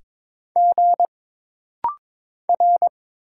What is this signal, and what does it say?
Morse code, heard as a keyed tone, about 22 wpm, 700 Hz. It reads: G R
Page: This rapid-fire timing is geared to push your ICR (Instant Character Recognition) skills.